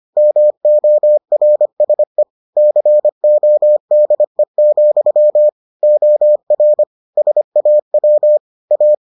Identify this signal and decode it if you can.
Morse code, heard as a keyed tone, about 25 wpm, 600 Hz. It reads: MORSE CODE, OR SAW A